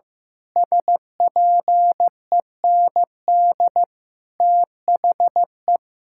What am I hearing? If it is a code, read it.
Morse code, heard as a keyed tone, about 15 wpm, 700 Hz. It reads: SPEND THE
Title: Sentences from Top 1000 Words - Review Part 5 - 15wpm